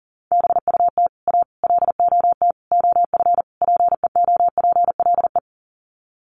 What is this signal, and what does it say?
Morse code, heard as a keyed tone, about 40 wpm, 700 Hz. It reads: BUT A LOT OF PEOPLE
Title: Sets of 5 Words Encore2 40wpm